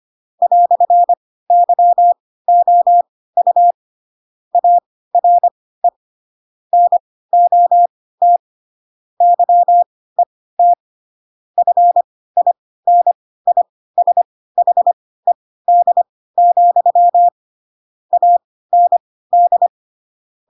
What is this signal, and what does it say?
Morse code, heard as a keyed tone, about 25 wpm, spaced as for 16 wpm, 700 Hz. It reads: "YOU ARE NOT YET FINISHED, AND
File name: Pinocchio / Pinocchio ch3 10 25 ik1hge